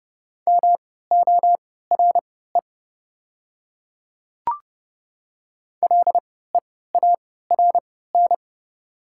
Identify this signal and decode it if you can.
Morse code, heard as a keyed tone, about 30 wpm, spaced as for 17 wpm, 690 Hz. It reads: MORE LEARN
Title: 3x Character Spacing - Top 300 Words - 30wpm